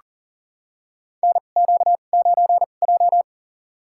Morse code, heard as a keyed tone, about 40 wpm, spaced as for 29 wpm, 700 Hz. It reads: NQ9J
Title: USA Call Signs - 2x Character Spacing - 40wpm